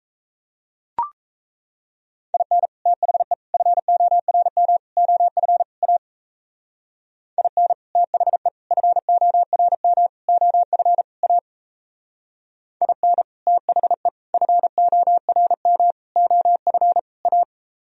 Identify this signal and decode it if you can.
Morse code, heard as a keyed tone, about 35 wpm, 700 Hz. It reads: IN THE FORM OF A IN THE FORM OF A IN THE FORM OF A
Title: Speed-Racing - Sets of 5 Words 28wpm